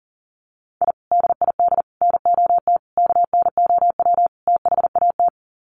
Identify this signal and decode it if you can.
Morse code, heard as a keyed tone, about 40 wpm, 700 Hz. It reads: I DID NOT KNOW THAT
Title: Sets of 5 Words 40wpm